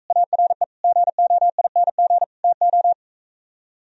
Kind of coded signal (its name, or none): Morse code